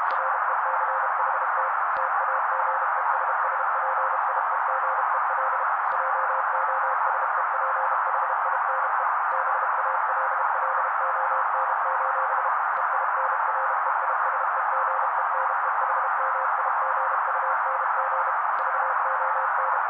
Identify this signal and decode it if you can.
Morse code, heard as a keyed tone, about 32 wpm, 530 Hz. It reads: RECHT TAGESSPIEGEL JOSEPHINE BALLON: EINREISEGENEHMIGUNG VON
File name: daily / 31wpm noise